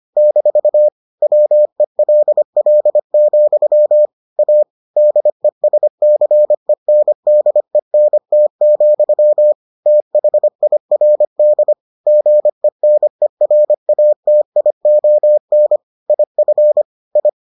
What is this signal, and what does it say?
Morse code, heard as a keyed tone, about 25 wpm, 600 Hz. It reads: - WELL, A DESCENDENT, THIRD GENERATION IF I